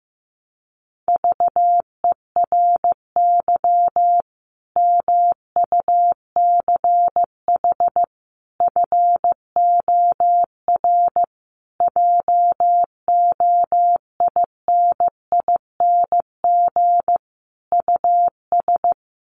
Morse code, heard as a keyed tone, about 15 wpm, 700 Hz. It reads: VERY MUCH FOR JOINING US
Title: Sets of 5 Words 15wpm